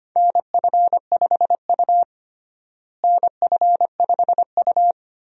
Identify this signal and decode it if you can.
Morse code, heard as a keyed tone, about 25 wpm, 700 Hz. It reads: NF5U NF5U
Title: USA Call Signs - 4 Characters - Repeated 3x - 25wpm